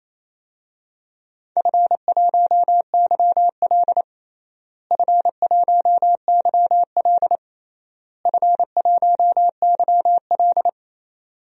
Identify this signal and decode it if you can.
Morse code, heard as a keyed tone, about 28 wpm, 700 Hz. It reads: F1YL F1YL F1YL
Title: DXCC-100 Call Signs - 4 Characters - Repeated 3x - 28wpm